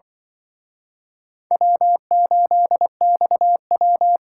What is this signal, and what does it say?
Morse code, heard as a keyed tone, about 24 wpm, 700 Hz. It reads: W8XW
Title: Speed Racing - USA 1x2 Call Signs 20wpm